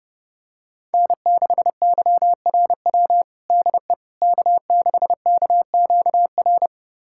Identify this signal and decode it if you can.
Morse code, heard as a keyed tone, about 30 wpm, 700 Hz. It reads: N6YRW DE K6KQR
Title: USA Callsign DE Callsign - 5 Characters - 30wpm